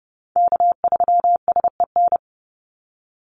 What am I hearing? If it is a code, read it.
Morse code, heard as a keyed tone, about 30 wpm, 700 Hz. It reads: K3SEN
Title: Call Signs Repeated 2x - 30wpm